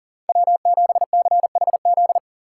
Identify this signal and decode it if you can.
Morse code, heard as a keyed tone, about 37 wpm, 700 Hz. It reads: W7CHZ
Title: USA Call Signs - 5 Characters - 4x Character Spacing - 40wpm